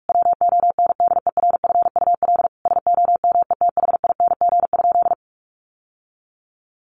Morse code, heard as a keyed tone, about 44 wpm, 700 Hz. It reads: WONDERFUL SOMETHING?
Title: Sentences from Top 1100 Words 45wpm